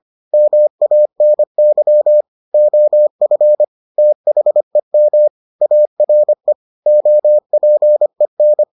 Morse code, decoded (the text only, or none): MANY OF THEM ARE OPEN